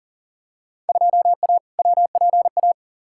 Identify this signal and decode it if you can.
Morse code, heard as a keyed tone, about 40 wpm, 700 Hz. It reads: 2A WPA
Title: Field Day Exchanges - 40wpm